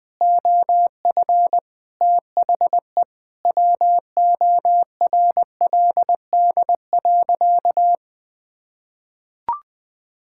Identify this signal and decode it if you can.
Morse code, heard as a keyed tone, about 21 wpm, 700 Hz. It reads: OF THE WORLD.
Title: Sentences from Top 1600 Words 20wpm